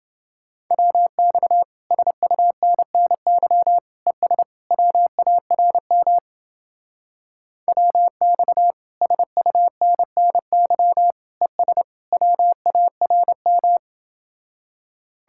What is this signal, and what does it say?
Morse code, heard as a keyed tone, about 31 wpm, 700 Hz. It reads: WX SUNNY ES WARM WX SUNNY ES WARM